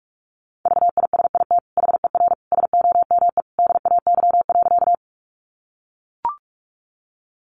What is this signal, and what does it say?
Morse code, heard as a keyed tone, about 44 wpm, 700 Hz. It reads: VISIT HER SOME DAY.
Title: Sentences from Top 900 Words 45wpm